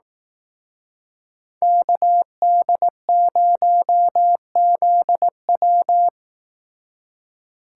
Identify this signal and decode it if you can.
Morse code, heard as a keyed tone, about 18 wpm, 700 Hz. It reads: KD0ZW
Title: Speed Racing - USA Call Signs 15wpm